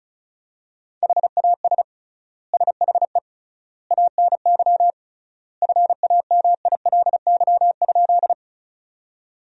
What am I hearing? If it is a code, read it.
Morse code, heard as a keyed tone, about 36 wpm, 700 Hz. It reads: HAS SHE ANY FAMILY?
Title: 3x Word-Spacing - Sentences from Top 400 Words - 35wpm